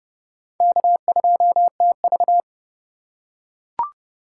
Morse code, heard as a keyed tone, about 30 wpm, 700 Hz. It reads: K2TV
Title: K1USN SST QSO Practice - 30wpm